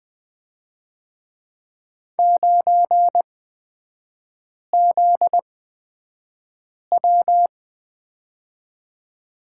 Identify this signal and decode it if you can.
Morse code, heard as a keyed tone, about 20 wpm, spaced as for 5 wpm, 700 Hz. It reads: 9ZW
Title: 243 Lesson 39e - Letter z - Three Characters 20wpm